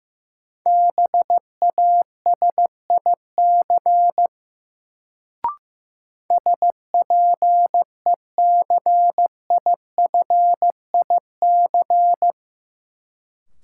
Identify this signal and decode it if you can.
Morse code, heard as a keyed tone, about 15 wpm, 700 Hz. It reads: BASIC SPECIFIC
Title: Sentences from Top 1000 Words - Review Part 3 - New Word Drill - 15wpm